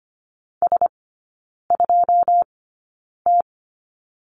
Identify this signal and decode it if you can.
Morse code, heard as a keyed tone, about 25 wpm, spaced as for 9 wpm, 700 Hz. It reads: S2T
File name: Lesson 17e - Letter u - Three Characters 25wpm